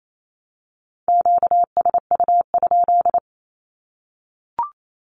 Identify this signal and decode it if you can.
Morse code, heard as a keyed tone, about 28 wpm, 700 Hz. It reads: QSU?
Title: Q-Codes Repeated 3x 28wpm